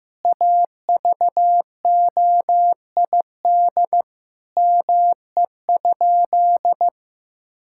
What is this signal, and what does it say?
Morse code, heard as a keyed tone, about 14 wpm, 700 Hz. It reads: AVOID ME?
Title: Sentences from Top 1000 Words - Review Part 6 - 15wpm